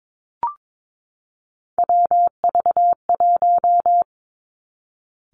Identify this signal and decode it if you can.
Morse code, heard as a keyed tone, about 22 wpm, 700 Hz. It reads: WV1
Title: Three Challenging Letter-Number Combinations 22wpm